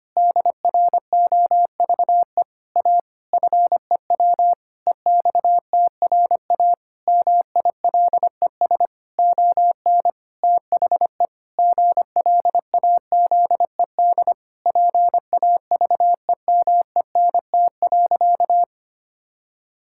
Morse code, heard as a keyed tone, about 25 wpm, 700 Hz. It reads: DROVE A FEW EXTRA MILES ON THE GLAZED PAVEMENT.
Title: Pangram Challenge v2 - 25wpm